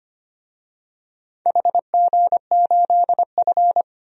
Morse code, heard as a keyed tone, about 25 wpm, 700 Hz. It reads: HG8F
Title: DXCC-100 Call Signs - 4 Characters - 25wpm